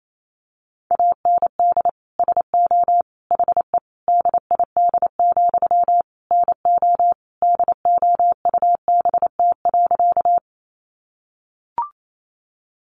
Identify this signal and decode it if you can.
Morse code, heard as a keyed tone, about 28 wpm, 700 Hz. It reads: AND SO HE DID, NO DOUBT.
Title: Sentences from Top 1500 Words 28wpm